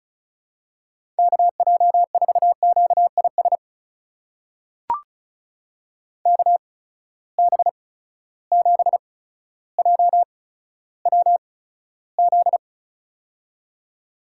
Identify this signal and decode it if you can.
Morse code, heard as a keyed tone, about 36 wpm, spaced as for 15 wpm, 700 Hz. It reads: KJ4QIS KB7JWZ
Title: USA Call Signs - 6 Characters - 8x Character Spacing - 35wpm